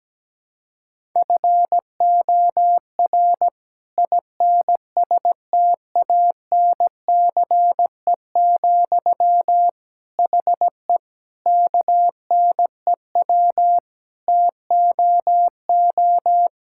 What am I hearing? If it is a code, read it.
Morse code, heard as a keyed tone, about 17 wpm, 700 Hz. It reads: FOR INSTANCE, HE KNEW TOO
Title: Sentences from Top 1000 Words - Review Part 5 - 17wpm